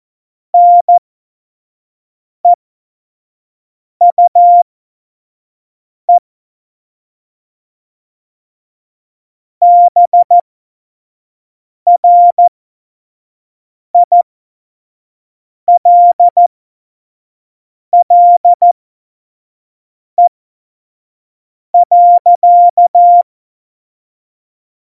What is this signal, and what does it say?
Morse code, heard as a keyed tone, about 14 wpm, spaced as for 5 wpm, 700 Hz. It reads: NEUE BRILLE.
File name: Morsekurs / Witze / Witz 14. November 2017, 25 F70